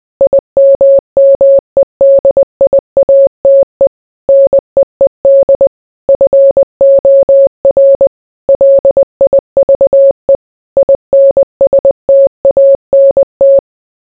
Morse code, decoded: IMMEDIATE NEED FOR LIVE INSTANT

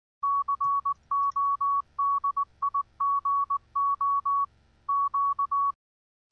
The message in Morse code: CODIGO Q

